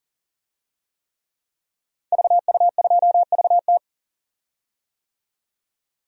Morse code, heard as a keyed tone, about 40 wpm, 700 Hz. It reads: VU2VT
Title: DXCC-100 Call Signs - 5 Characters - 40wpm